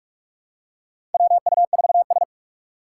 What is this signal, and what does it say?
Morse code, heard as a keyed tone, about 46 wpm, 700 Hz. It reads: WU4S
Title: USA Call Signs - 4 Characters - 6x Character Spacing - 45wpm